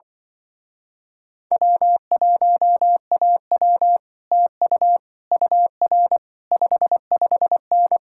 Morse code, heard as a keyed tone, about 24 wpm, 700 Hz. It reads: W1AW TU UR 55N
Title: Speed-Racing Common QSO Phrases Encore - 20wpm